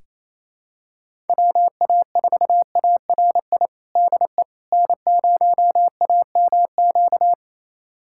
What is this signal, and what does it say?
Morse code, heard as a keyed tone, about 28 wpm, 700 Hz. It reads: WA4ARI DE N0AMQ
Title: USA Callsign DE Callsign 28wpm